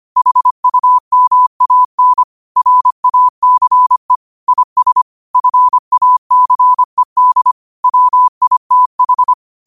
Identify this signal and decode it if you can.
Morse code, heard as a keyed tone, about 25 wpm, 1000 Hz. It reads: HUMAN RACE IS FACED WITH
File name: qotdfiles / 25wpm